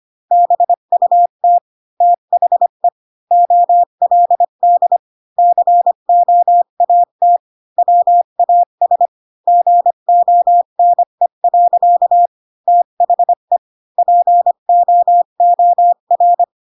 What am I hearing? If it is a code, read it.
Morse code, heard as a keyed tone, about 25 wpm, 700 Hz. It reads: BUT THE OLD COAT WAS GONE. THE POOR
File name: Pinocchio / Pinocchio ch8 20 25 ik1hge